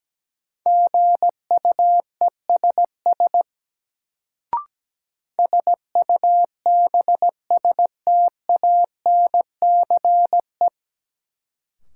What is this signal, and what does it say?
Morse code, heard as a keyed tone, about 17 wpm, 700 Hz. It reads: GUESS SUBSTANCE
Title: Sentences from Top 1500 Words - New Word Drill - 17wpm